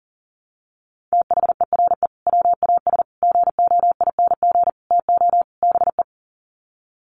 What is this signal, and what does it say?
Morse code, heard as a keyed tone, about 40 wpm, 700 Hz. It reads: THERE WAS GOING TO BE
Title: Sets of 5 Words Encore2 40wpm